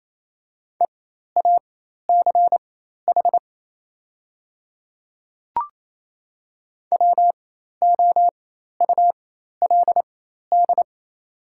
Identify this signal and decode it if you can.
Morse code, heard as a keyed tone, about 28 wpm, spaced as for 13 wpm, 700 Hz. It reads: EACH WOULD
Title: 4x Character Spacing - Top 100 Words - 28wpm